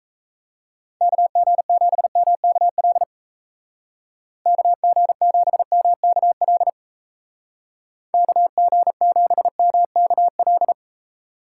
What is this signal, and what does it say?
Morse code, heard as a keyed tone, about 35 wpm, 700 Hz. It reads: KG7MKL KG7MKL KG7MKL
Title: Speed Racing - USA Call Signs 28wpm